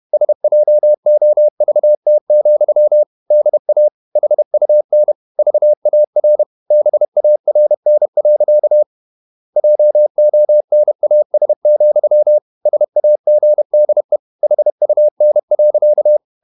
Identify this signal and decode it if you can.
Morse code, heard as a keyed tone, about 31 wpm, 600 Hz. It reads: SJOVT, DA HUN VAR BARN. JONAS, SAGDE HUN.
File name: Ferie QRQ 31wpm